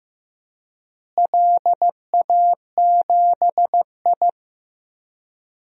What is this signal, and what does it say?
Morse code, heard as a keyed tone, about 15 wpm, 700 Hz. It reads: LA7I
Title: DXCC-100 Call Signs - 4 Characters - Repeated 3x - 15wpm